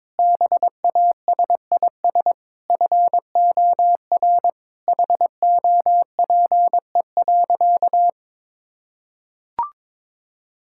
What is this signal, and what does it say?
Morse code, heard as a keyed tone, about 22 wpm, 700 Hz. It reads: BASIS FOR HOPE.
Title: Sentences from Top 1000 Words - Review Part 5 - 22wpm